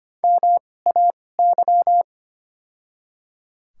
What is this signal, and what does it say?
Morse code, heard as a keyed tone, about 25 wpm, spaced as for 18 wpm, 700 Hz. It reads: MAY
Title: 2x Character Spacing - Top 200 Words - 25wpm